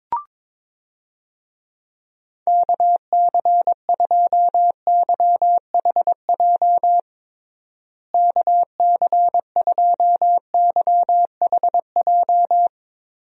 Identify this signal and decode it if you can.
Morse code, heard as a keyed tone, about 22 wpm, 700 Hz. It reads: KC2YHJ KC2YHJ
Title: USA Call Signs - 6 Characters - Repeated 3x - 22wpm